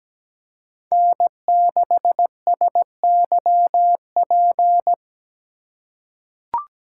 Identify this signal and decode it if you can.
Morse code, heard as a keyed tone, about 17 wpm, 700 Hz. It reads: N6SYP
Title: USA Call Signs - 5 Characters - 2x Character Spacing - 17wpm